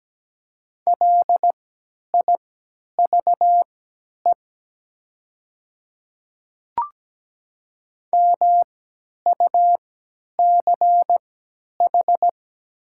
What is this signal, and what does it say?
Morse code, heard as a keyed tone, about 17 wpm, spaced as for 10 wpm, 700 Hz. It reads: LIVE MUCH
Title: 3x Character Spacing - Top 200 Words - 17wpm